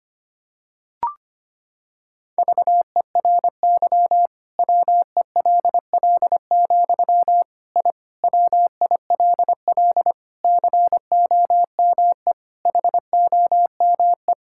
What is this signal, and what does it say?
Morse code, heard as a keyed tone, about 25 wpm, 700 Hz. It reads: VERY WELL, I WILL COME HOME
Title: Sentences from Top 1000 Words - Review Part 5 - 25wpm